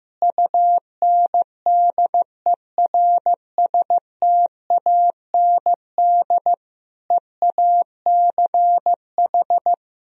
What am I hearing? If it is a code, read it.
Morse code, heard as a keyed tone, about 15 wpm, 700 Hz. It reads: UNDERSTAND EACH
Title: Sentences from Top 1000 Words - Review Part 1 - 15wpm